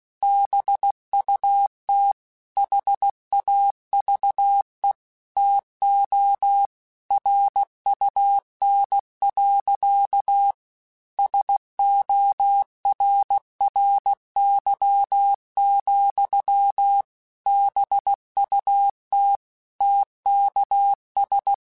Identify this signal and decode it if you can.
Morse code, heard as a keyed tone, about 16 wpm, 800 Hz. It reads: BUT HAVE TO RUN. SORRY, BUT TKS
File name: QSOs / QSOs-mp3s / kc7sac@15